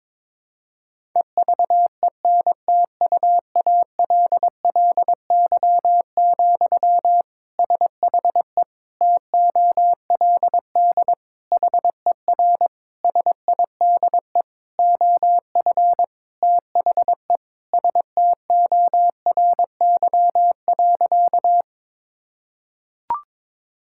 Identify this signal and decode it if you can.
Morse code, heard as a keyed tone, about 22 wpm, 700 Hz. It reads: EVENTUALLY, SHE TOLD HER SIDE OF THE STORY.
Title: Sentences from Top 1000 Words - Review Part 6 - 22wpm